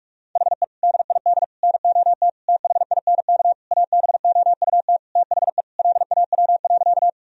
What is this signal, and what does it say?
Morse code, heard as a keyed tone, about 46 wpm, 700 Hz. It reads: HE DID NOT THINK ABOUT THE LAW.
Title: Sentences from Top 1000 Words - Review Part 3 - 45wpm